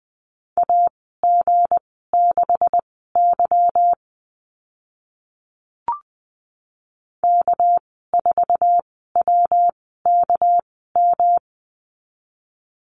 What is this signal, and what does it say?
Morse code, heard as a keyed tone, about 20 wpm, spaced as for 14 wpm, 700 Hz. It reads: AG6Y K4WKM
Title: 200 Lesson 32g - Letter k - Call Signs 20wpm